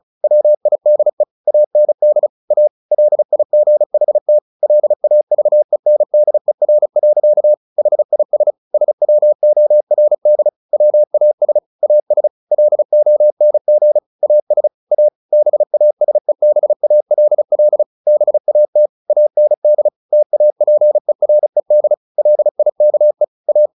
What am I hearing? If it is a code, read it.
Morse code, heard as a keyed tone, about 34 wpm, 600 Hz. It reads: WIDE AND A LIGHT LAVENDER. HIS SWORD WAS AS LONG AS A BASEBALL BAT AND TAPERED LIKE A
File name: the old man and the sea / the old man and the sea 1 35wpm